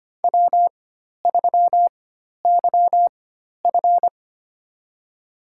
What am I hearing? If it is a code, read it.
Morse code, heard as a keyed tone, about 25 wpm, spaced as for 12 wpm, 700 Hz. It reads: W3YF
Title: USA Call Signs - 4 Characters - 4x Character Spacing - 25wpm